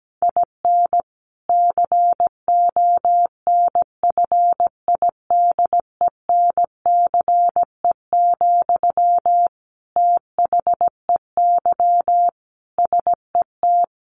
Morse code, heard as a keyed tone, about 17 wpm, 700 Hz. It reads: IN CONFIDENCE, THEY SET
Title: Sentences from Top 1500 Words 17wpm